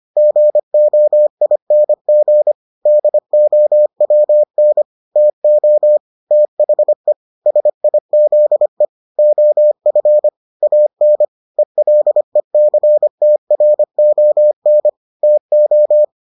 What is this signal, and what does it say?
Morse code, heard as a keyed tone, about 25 wpm, 600 Hz. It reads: GOING DOWN TO THE SIZE OF AN ELECTRON TO